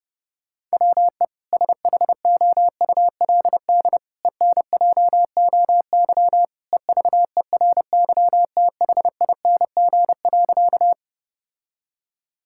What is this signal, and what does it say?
Morse code, heard as a keyed tone, about 30 wpm, 700 Hz. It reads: WE SHOULD ENJOY EVERYTHING.
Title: Sentences from Top 1000 Words - Review Part 5 - 30wpm